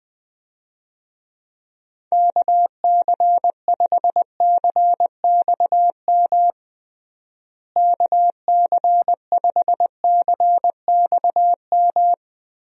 Morse code, heard as a keyed tone, about 20 wpm, 700 Hz. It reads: KC5CXM KC5CXM
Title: USA Call Signs - 6 Characters - Repeated 3x - 20wpm